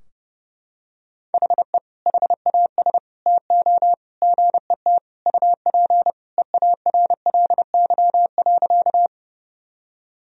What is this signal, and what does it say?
Morse code, heard as a keyed tone, about 30 wpm, 700 Hz. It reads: HE HAS TO GET UP EARLY.
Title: Sentences from Top 1000 Words - Review Part 2 - 30wpm